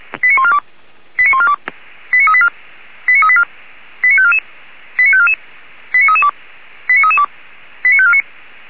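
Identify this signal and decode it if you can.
Five-tone selective call (ZVEI), heard as quick runs of five tones, tones of about 70 ms. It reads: 89142 89142 89384 89384 89544 89544 89332 89332 89549
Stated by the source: ZVEI five-tone call received by SDR, Feuerwehr Orlamünde.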